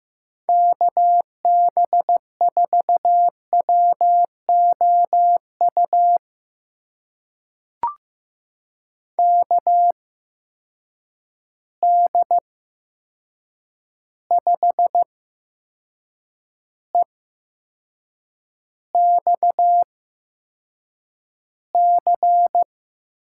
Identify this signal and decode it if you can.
Morse code, heard as a keyed tone, about 15 wpm, spaced as for 6 wpm, 700 Hz. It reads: KB4WOU KD5EXC
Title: USA Call Signs - 6 Characters - 8x Character Spacing - 15wpm